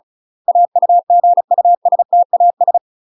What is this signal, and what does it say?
Morse code, heard as a keyed tone, about 36 wpm, 700 Hz. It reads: AUGUSTAS
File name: UK MASTER QSOs TRAINING / 35WPM / UK MASTER QSOs 0007